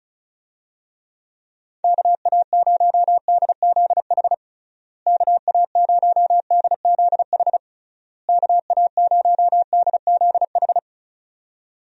Morse code, heard as a keyed tone, about 36 wpm, 700 Hz. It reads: KA0DZH KA0DZH KA0DZH
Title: USA Call Signs - 6 Characters - Repeated 3x - 35wpm